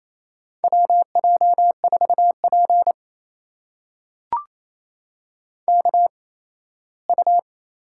Morse code, heard as a keyed tone, about 28 wpm, spaced as for 17 wpm, 700 Hz. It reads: WJ4P KU
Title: USA Call Signs - 4 Characters - 8x Character Spacing - 28wpm